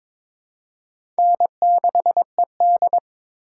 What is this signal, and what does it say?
Morse code, heard as a keyed tone, about 21 wpm, 700 Hz. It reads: N6ED
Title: USA Call Signs - 4 Characters - 4x Character Spacing - 22wpm